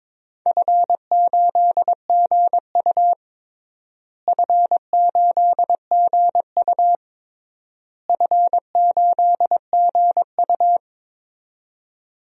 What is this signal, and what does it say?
Morse code, heard as a keyed tone, about 22 wpm, 700 Hz. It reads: F8GU F8GU F8GU
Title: DXCC-100 Call Signs - 4 Characters - Repeated 3x - 22wpm